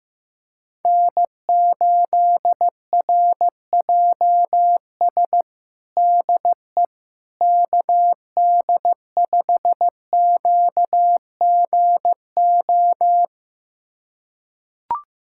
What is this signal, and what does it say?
Morse code, heard as a keyed tone, about 16 wpm, 700 Hz. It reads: N8RJS DE KD5QGO
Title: USA Callsign DE Callsign 15wpm